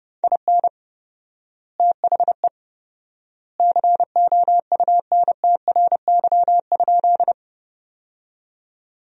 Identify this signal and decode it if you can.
Morse code, heard as a keyed tone, about 30 wpm, 700 Hz. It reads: IN THE COUNTRY?
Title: 4x Word-Spacing - Sentences from Top 300 Words - 30wpm